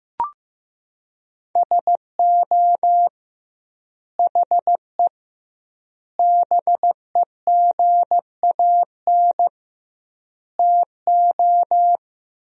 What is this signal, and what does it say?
Morse code, heard as a keyed tone, about 16 wpm, 700 Hz. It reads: SO HE BEGAN TO
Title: 2x Word-Spacing - Sentences from Top 300 Words - 15wpm